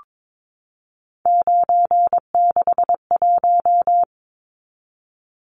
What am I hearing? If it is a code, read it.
Morse code, heard as a keyed tone, about 22 wpm, 700 Hz. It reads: 961
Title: Numbers - 3-Digit 22wpm